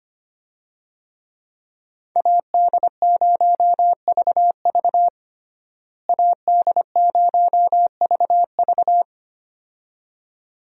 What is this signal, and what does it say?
Morse code, heard as a keyed tone, about 25 wpm, 700 Hz. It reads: AD0VV AD0VV
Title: USA Call Signs - 5 Characters - Repeated 2x - 25wpm